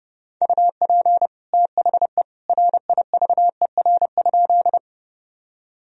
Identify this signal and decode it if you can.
Morse code, heard as a keyed tone, about 30 wpm, 700 Hz. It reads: UP THE RIVER?
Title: Sentences from Top 1000 Words - Review Part 1 - 30wpm